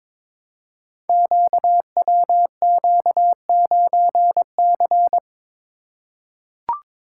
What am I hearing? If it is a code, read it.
Morse code, heard as a keyed tone, about 22 wpm, 700 Hz. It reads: QWQ9C
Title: Code Groups - 5 Letter:Numbers - 22wpm